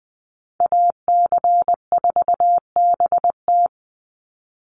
Morse code, heard as a keyed tone, about 20 wpm, 700 Hz. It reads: AC4BT
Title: Call Signs - 20wpm